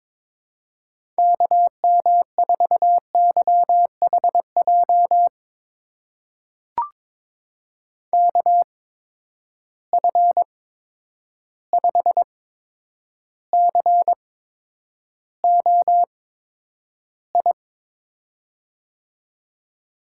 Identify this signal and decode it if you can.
Morse code, heard as a keyed tone, about 22 wpm, spaced as for 9 wpm, 700 Hz. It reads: KM4YHJ KF5COI
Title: USA Call Signs - 6 Characters - 8x Character Spacing - 22wpm